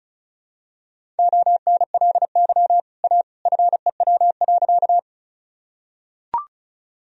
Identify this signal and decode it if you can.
Morse code, heard as a keyed tone, about 36 wpm, 700 Hz. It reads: ONLY A FEW.
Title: Sentences from Top 1000 Words - Review Part 1 - 35wpm